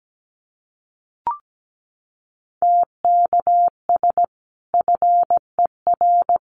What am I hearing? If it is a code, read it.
Morse code, heard as a keyed tone, about 17 wpm, 700 Hz. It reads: TKS FER
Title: Common QSO Phrases 17wpm